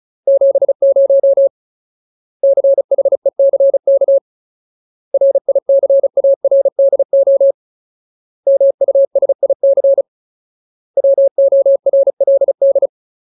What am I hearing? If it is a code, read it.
Morse code, heard as a keyed tone, about 34 wpm, 550 Hz. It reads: 70 CHECK RICARDO MUSIC WORLD
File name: Ultimate Word List / 35WPM Ultimate